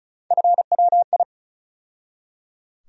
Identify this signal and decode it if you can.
Morse code, heard as a keyed tone, about 34 wpm, 700 Hz. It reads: FWI
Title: USA Call Sign Postfixes - 3 Letters 35wpm